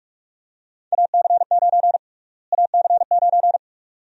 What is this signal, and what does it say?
Morse code, heard as a keyed tone, about 41 wpm, 700 Hz. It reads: AC9 AC9
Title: USA Call Sign Prefixes - Repeated 2x - 45wpm